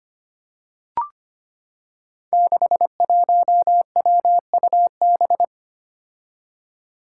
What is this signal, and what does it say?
Morse code, heard as a keyed tone, about 25 wpm, 700 Hz. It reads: 61WUB
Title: Five Challenging Letter-Number Combinations 25wpm